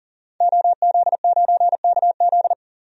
Morse code, heard as a keyed tone, about 38 wpm, 700 Hz. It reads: OZ9KZ
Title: DXCC-100 Call Signs - 5 Characters - 40wpm